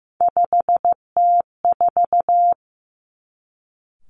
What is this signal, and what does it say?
Morse code, heard as a keyed tone, about 15 wpm, 700 Hz. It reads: 5T4
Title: Three Challenging Letter-Number Combinations 15wpm